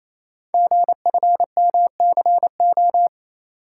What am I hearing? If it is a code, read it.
Morse code, heard as a keyed tone, about 28 wpm, 700 Hz. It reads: GFMCO
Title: Code Groups - 5 Letters - 28wpm